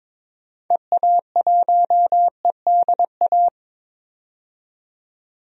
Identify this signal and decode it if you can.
Morse code, heard as a keyed tone, about 22 wpm, 700 Hz. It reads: EA1EDA
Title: DXCC-100 Call Signs - 6 Characters - 22wpm